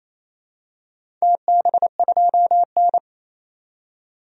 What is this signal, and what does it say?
Morse code, heard as a keyed tone, about 28 wpm, 700 Hz. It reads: TB2N
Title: DXCC-100 Call Signs - 4 Characters - 28wpm